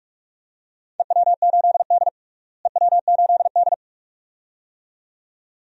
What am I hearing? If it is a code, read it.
Morse code, heard as a keyed tone, about 46 wpm, 700 Hz. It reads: EW8D EW8D
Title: DXCC-100 Call Signs - 4 Characters - Repeated 2x - 45wpm